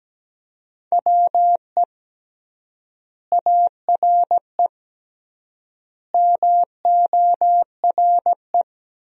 Morse code, heard as a keyed tone, about 17 wpm, 700 Hz. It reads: WE ARE MORE
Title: 3x Word-Spacing - Sentences from Top 500 Words - 17wpm